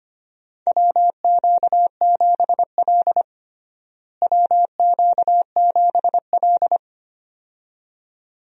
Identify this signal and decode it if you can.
Morse code, heard as a keyed tone, about 25 wpm, 700 Hz. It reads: WQ7L WQ7L
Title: USA Call Signs - 4 Characters - Repeated 2x - 25wpm